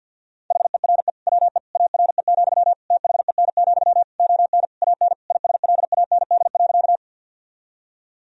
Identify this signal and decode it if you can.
Morse code, heard as a keyed tone, about 50 wpm, 700 Hz. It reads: HERE WE ARE, THEN, ON AN ISLAND.
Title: Sentences from Top 1000 Words - Review Part 3 - 50wpm